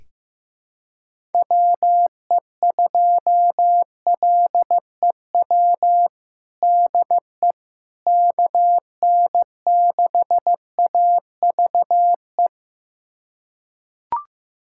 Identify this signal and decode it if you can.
Morse code, heard as a keyed tone, about 15 wpm, 700 Hz. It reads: WE2LEW DE KN6AVE
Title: USA Callsign DE Callsign - 6 Characters - 15wpm